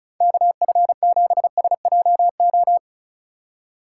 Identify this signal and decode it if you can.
Morse code, heard as a keyed tone, about 44 wpm, 700 Hz. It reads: KF7SJO